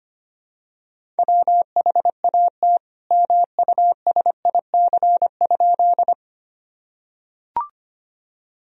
Morse code, heard as a keyed tone, about 25 wpm, 700 Hz. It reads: WHAT MUSIC?
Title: Sentences from Top 1000 Words - Review Part 1 - 25wpm